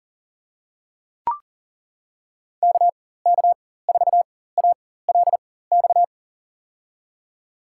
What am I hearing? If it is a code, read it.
Morse code, heard as a keyed tone, about 40 wpm, spaced as for 19 wpm, 700 Hz. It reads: KK4ALX
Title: USA Call Signs - 4x Character Spacing - 40wpm